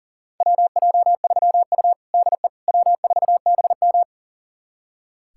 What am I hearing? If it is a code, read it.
Morse code, heard as a keyed tone, about 37 wpm, 700 Hz. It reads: WJ3U DE W4BM
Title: USA Callsign DE Callsign - 4 Characters - 40wpm